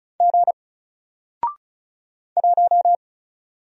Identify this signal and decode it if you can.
Morse code, heard as a keyed tone, about 36 wpm, 700 Hz. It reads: G 1
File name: Single-Letter-Number-35wpm